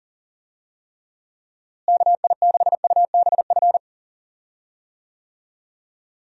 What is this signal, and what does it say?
Morse code, heard as a keyed tone, about 40 wpm, 700 Hz. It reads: KI6UBF
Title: USA Call Signs - 6 Characters - Repeated 2x - 40wpm